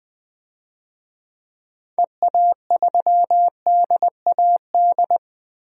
Morse code, heard as a keyed tone, about 21 wpm, 700 Hz. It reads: EA3DAD
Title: DXCC-100 Call Signs - 6 Characters - Repeated 3x - 20wpm